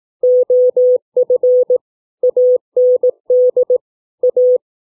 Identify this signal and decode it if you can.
Morse code, heard as a keyed tone, about 18 wpm, 500 Hz. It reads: OF AND A